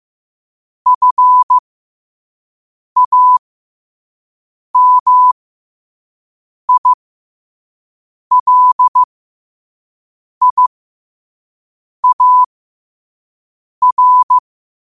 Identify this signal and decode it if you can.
Morse code, heard as a keyed tone, about 15 wpm, spaced as for 5 wpm, 1000 Hz. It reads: FAMILIAR